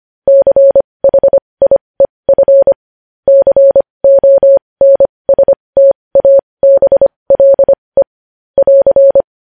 Morse code, heard as a keyed tone, about 25 wpm, spaced as for 20 wpm, 550 Hz. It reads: CHIEF CONSTABLE <AR>
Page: News headlines in Morse code at 20 wpm